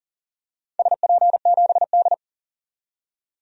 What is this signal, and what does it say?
Morse code, heard as a keyed tone, about 40 wpm, 700 Hz. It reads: SP7D
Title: DXCC-100 Call Signs - 4 Characters - 40wpm